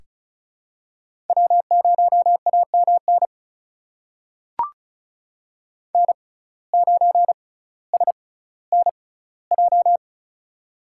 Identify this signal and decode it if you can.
Morse code, heard as a keyed tone, about 36 wpm, spaced as for 18 wpm, 700 Hz. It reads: W0AMN N9SNJ
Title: USA Call Signs - 5 Characters - 6x Character Spacing - 35wpm